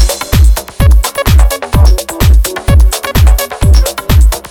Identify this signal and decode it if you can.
There is no code.